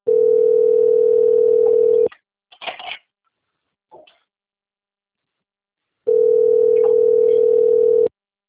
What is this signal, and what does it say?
Ringback tone (North America), 440 + 480 Hz.
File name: out- - - - - .70021